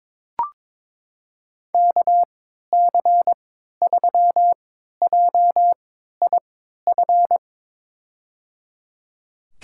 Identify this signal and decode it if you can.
Morse code, heard as a keyed tone, about 22 wpm, spaced as for 13 wpm, 700 Hz. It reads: KC3JIF
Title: USA Call Signs - 6 Characters - 3x Character Spacing - 22wpm